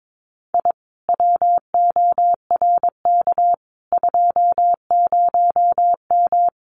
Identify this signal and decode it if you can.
Morse code, heard as a keyed tone, about 22 wpm, 700 Hz. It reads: I WORK 20M
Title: Common QSO Phrases Encore 22wpm